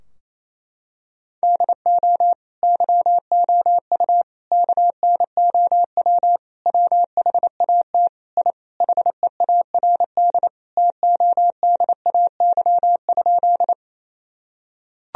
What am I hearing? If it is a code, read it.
Morse code, heard as a keyed tone, about 28 wpm, 700 Hz. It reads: DO YOU KNOW WHAT I HEARD TODAY?
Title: Sentences from Top 1000 Words - Review Part 4 - 28wpm